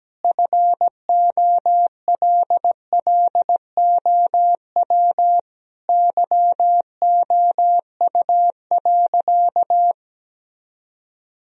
Morse code, heard as a keyed tone, about 17 wpm, 700 Hz. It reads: FOLLOW YOU.
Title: Sentences from Top 1000 Words - Review Part 1 - 17wpm